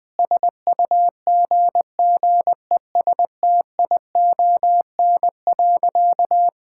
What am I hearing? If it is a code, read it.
Morse code, heard as a keyed tone, about 20 wpm, 700 Hz. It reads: SUGGESTION.
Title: Sentences from Top 1600 Words 20wpm